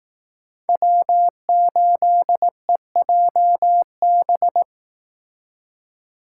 Morse code, heard as a keyed tone, about 18 wpm, 700 Hz. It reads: W8EJB
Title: Speed Racing - USA 1x3 Call Signs 15wpm